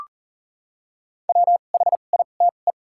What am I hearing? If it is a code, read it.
Morse code, heard as a keyed tone, about 40 wpm, 700 Hz. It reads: W H I T E
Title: 2x Character Spacing - Top 300 Words - 40wpm